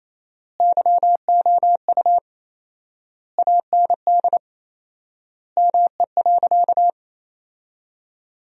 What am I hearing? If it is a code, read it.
Morse code, heard as a keyed tone, about 28 wpm, 700 Hz. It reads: YOU AND ME.
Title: 4x Word-Spacing - Sentences from Top 500 Words - 28wpm